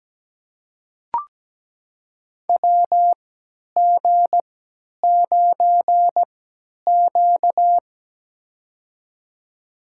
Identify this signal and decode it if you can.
Morse code, heard as a keyed tone, about 17 wpm, spaced as for 10 wpm, 700 Hz. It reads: WG9Q
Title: USA Call Signs - 3x Character Spacing - 17wpm